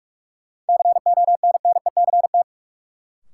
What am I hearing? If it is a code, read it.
Morse code, heard as a keyed tone, about 46 wpm, 700 Hz. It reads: CONNECT
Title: Sentences from Top 1500 Words - New Word Drill Alt - 45wpm